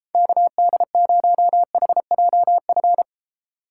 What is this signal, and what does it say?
Morse code, heard as a keyed tone, about 31 wpm, 700 Hz. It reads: KD0HJF